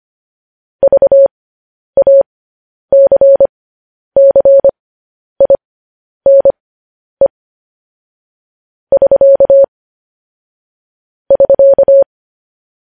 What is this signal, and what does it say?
Morse code, heard as a keyed tone, about 25 wpm, spaced as for 10 wpm, 550 Hz. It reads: VACCINE <SK> <SK>
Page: news-headlines-in-morse-code-at-10-wpm